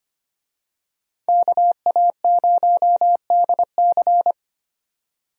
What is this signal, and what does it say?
Morse code, heard as a keyed tone, about 25 wpm, 700 Hz. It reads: KA0DC
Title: USA Call Signs - 5 Characters - 25wpm